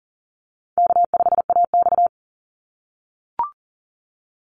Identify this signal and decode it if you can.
Morse code, heard as a keyed tone, about 40 wpm, 700 Hz. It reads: K5AX
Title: Call Signs - 40wpm